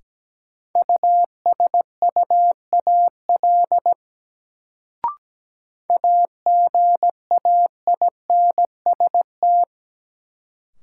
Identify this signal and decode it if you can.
Morse code, heard as a keyed tone, about 17 wpm, 700 Hz. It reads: USUAL AGAINST
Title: Sentences from Top 400 Words - New Word Drill - 17wpm